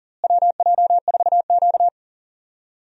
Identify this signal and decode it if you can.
Morse code, heard as a keyed tone, about 40 wpm, 700 Hz. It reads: WJ4Q
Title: USA Call Signs - 4 Characters - Repeated 3x - 40wpm